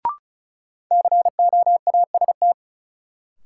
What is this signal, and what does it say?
Morse code, heard as a keyed tone, about 36 wpm, 700 Hz. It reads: COAST
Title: Sentences from Top 1300 Words - New Word Drill Alt - 35wpm